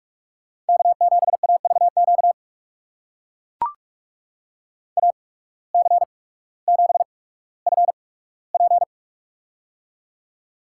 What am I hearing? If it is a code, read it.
Morse code, heard as a keyed tone, about 42 wpm, spaced as for 19 wpm, 700 Hz. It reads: K7AVQ AC7FP
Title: USA Call Signs - 5 Characters - 8x Character Spacing - 45wpm